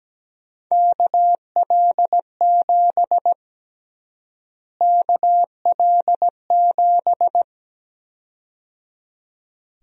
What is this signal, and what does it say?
Morse code, heard as a keyed tone, about 17 wpm, 700 Hz. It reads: KL7 KL7
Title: USA Call Sign Prefixes - Repeated 2x - 17wpm